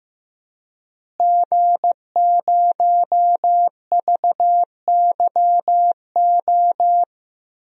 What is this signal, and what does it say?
Morse code, heard as a keyed tone, about 15 wpm, 700 Hz. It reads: G0VYO
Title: DXCC-100 Call Signs - 5 Characters - Repeated 3x - 15wpm